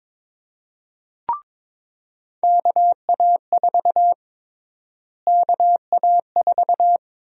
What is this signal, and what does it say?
Morse code, heard as a keyed tone, about 21 wpm, 700 Hz. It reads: KA4 KA4
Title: USA Call Sign Prefixes - Repeated 2x - 22wpm